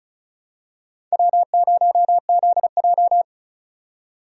Morse code, heard as a keyed tone, about 34 wpm, 700 Hz. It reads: W0ZJ
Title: USA Call Signs - 4 Characters - 35wpm